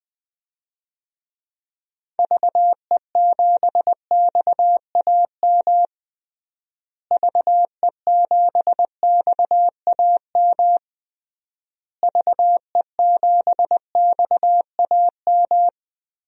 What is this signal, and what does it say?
Morse code, heard as a keyed tone, about 20 wpm, 700 Hz. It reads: VE7XAM VE7XAM VE7XAM
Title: DXCC-100 Call Signs - 6 Characters - Repeated 3x - 20wpm